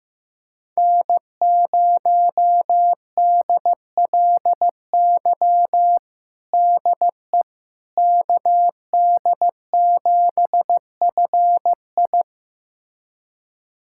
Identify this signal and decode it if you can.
Morse code, heard as a keyed tone, about 15 wpm, 700 Hz. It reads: N0DLY DE KD7FI
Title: USA Callsign DE Callsign - 5 Characters - 15wpm